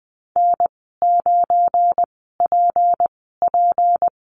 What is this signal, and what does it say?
Morse code, heard as a keyed tone, about 20 wpm, spaced as for 15 wpm, 700 Hz. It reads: N9PP
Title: 186 Lesson 30g - Letter b - Call Signs 20wpm